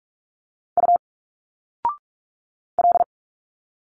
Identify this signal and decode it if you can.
Morse code, heard as a keyed tone, about 46 wpm, 700 Hz. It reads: U L
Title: Single Letters 45wpm